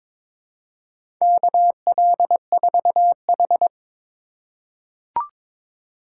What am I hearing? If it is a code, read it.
Morse code, heard as a keyed tone, about 23 wpm, 700 Hz. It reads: KL4H
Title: USA Call Signs - 4 Characters - 2x Character Spacing - 22wpm